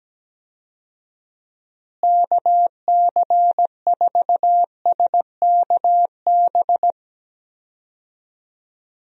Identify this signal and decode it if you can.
Morse code, heard as a keyed tone, about 17 wpm, 700 Hz. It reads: KC4SKB
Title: USA Call Signs - 6 Characters - Repeated 2x - 17wpm